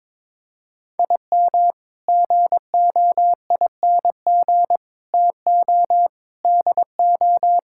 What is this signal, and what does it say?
Morse code, heard as a keyed tone, about 22 wpm, 700 Hz. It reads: IM GOING TO DO
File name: practice / 22wpm / Sentences with 8-Word Phrases - 22wpm